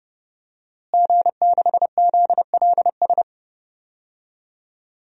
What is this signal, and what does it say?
Morse code, heard as a keyed tone, about 32 wpm, 700 Hz. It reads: G6ZLS
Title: DXCC-100 Call Signs - 5 Characters - Repeated 3x - 30wpm